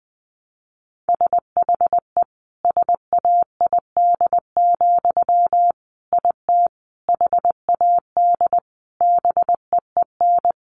Morse code, heard as a keyed tone, about 20 wpm, 700 Hz. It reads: SHE SAID, IT HAD BEEN
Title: Sentences from Top 800 Words 20wpm